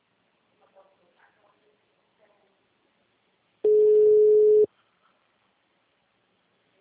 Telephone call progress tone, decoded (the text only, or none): ringback tone